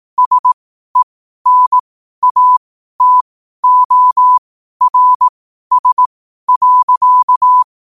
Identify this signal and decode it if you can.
Morse code, heard as a keyed tone, about 18 wpm, spaced as for 12 wpm, 1000 Hz. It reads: SENATORS.
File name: qotdfiles / 12wpm